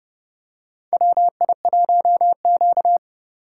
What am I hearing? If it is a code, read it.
Morse code, heard as a keyed tone, about 32 wpm, 700 Hz. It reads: WI1Q
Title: USA Call Signs - 3x Character Spacing - 30wpm